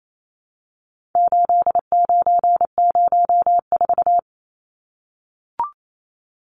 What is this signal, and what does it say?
Morse code, heard as a keyed tone, about 28 wpm, 700 Hz. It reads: 8904
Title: Numbers - 4-Digit 28wpm